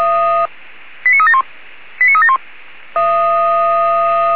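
Five-tone selective call (ZVEI), heard as quick runs of five tones, tones of about 70 ms. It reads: 89371 89371
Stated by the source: ZVEI five-tone call received by SDR.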